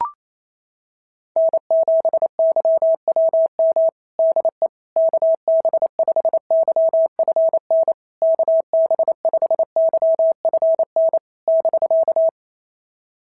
Morse code, heard as a keyed tone, about 28 wpm, 650 Hz. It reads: N7YWM DE KB5YFN KB5YFN <BK>
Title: SOTA QSO Practice 28wpm